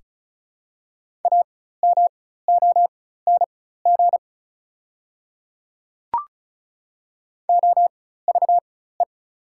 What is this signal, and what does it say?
Morse code, heard as a keyed tone, about 36 wpm, spaced as for 16 wpm, 700 Hz. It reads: AMONG OVE
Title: 4x Character Spacing - Top 500 Words - 35wpm